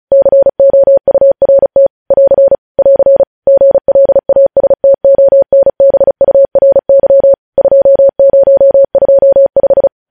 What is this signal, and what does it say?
Morse code, heard as a keyed tone, about 36 wpm, 550 Hz. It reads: COURT <AR> <AR> GLASTONBURY 2025
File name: podcast / News headlines-1x-35x35